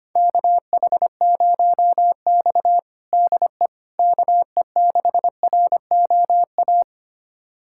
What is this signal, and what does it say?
Morse code, heard as a keyed tone, about 25 wpm, 700 Hz. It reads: KH0X DE KE6ROA
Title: USA Callsign DE Callsign 25wpm